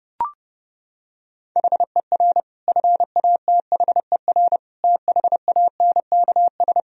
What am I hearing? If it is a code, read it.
Morse code, heard as a keyed tone, about 32 wpm, 700 Hz. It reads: HER FATHER THANKS
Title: Sentences from Top 1000 Words - Review Part 3 - 30wpm